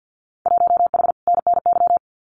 Morse code, heard as a keyed tone, about 48 wpm, 700 Hz. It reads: 1H NNY
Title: Winter Field Day Exchanges 50wpm